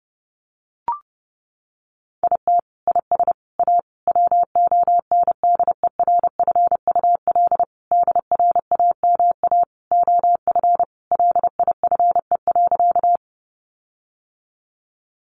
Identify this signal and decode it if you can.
Morse code, heard as a keyed tone, about 30 wpm, 700 Hz. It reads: IT IS A WONDERFUL DRAMA OF LIFE.
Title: Sentences from Top 1500 Words 30wpm